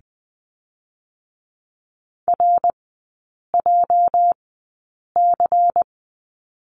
Morse code, heard as a keyed tone, about 20 wpm, spaced as for 8 wpm, 690 Hz. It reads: RJC
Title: 205 Lesson 33e - Letter j - Three Characters 20wpm